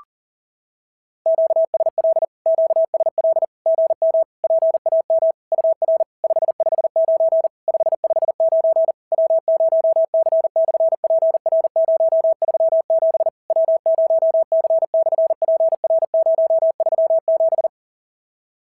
Morse code, heard as a keyed tone, about 40 wpm, 650 Hz. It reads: QSL QSL GM PAM UR 559 559 W0C/PR037 W0C/PR037
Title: SOTA QSO Practice 40wpm